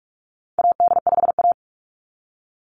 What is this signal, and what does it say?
Morse code, heard as a keyed tone, about 44 wpm, 700 Hz. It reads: AD5A
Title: NA SOTA Activator Callsigns 45wpm